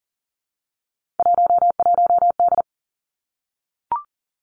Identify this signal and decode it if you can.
Morse code, heard as a keyed tone, about 41 wpm, 700 Hz. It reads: 11D
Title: Three Challenging Letter-Number Combinations 40wpm